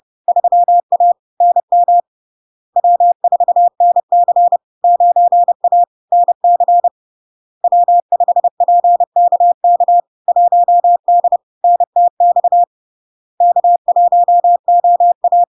Morse code, heard as a keyed tone, about 30 wpm, 700 Hz. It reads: 3A NM W4NC 9A NC W5PKK 1D NTX K1OA
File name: LARK-FD-2022-Log-in-CW-30wpm-700-Hz